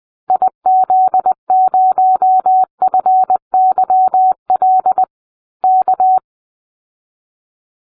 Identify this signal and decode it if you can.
Morse code, heard as a keyed tone, about 20 wpm, 750 Hz. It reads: IZ0FYL K